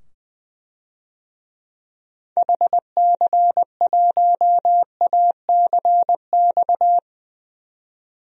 Morse code, heard as a keyed tone, about 21 wpm, 700 Hz. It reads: HC1ACX
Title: DXCC-100 Call Signs - 6 Characters - 20wpm